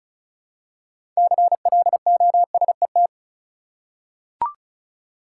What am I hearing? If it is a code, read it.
Morse code, heard as a keyed tone, about 34 wpm, 700 Hz. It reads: CLOSET
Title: Six Letter Words - 35wpm